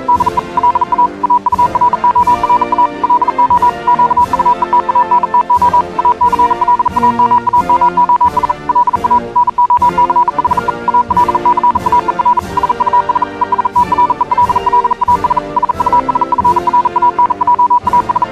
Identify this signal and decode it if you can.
Morse code, heard as a keyed tone, about 40 wpm, 980 Hz. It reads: DE LA AURORA LO QUE TANTO ACLAMAMOS LA NOCHE AL CAER SUS ESTRELLAS SUS FRANJAS
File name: morse hymne(1)